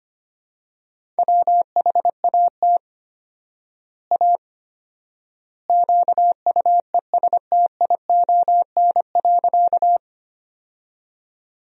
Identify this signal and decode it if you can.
Morse code, heard as a keyed tone, about 25 wpm, 700 Hz. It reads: WHAT A QUESTION.
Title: 4x Word-Spacing - Sentences from Top 400 Words - 25wpm